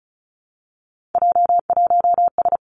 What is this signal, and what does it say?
Morse code, heard as a keyed tone, about 36 wpm, 700 Hz. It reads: J1S
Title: Three Challenging Letter-Number Combinations 35wpm